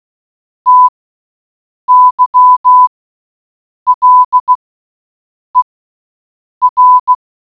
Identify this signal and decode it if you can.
Morse code, heard as a keyed tone, about 16 wpm, spaced as for 7 wpm, 1000 Hz. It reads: TYLER
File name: Role play group